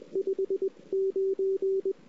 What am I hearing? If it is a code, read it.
Morse code, heard as a keyed tone, about 19 wpm, 380 Hz. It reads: 59
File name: CW / 2008-ss-cw-k5tr-op-n5rz -05:00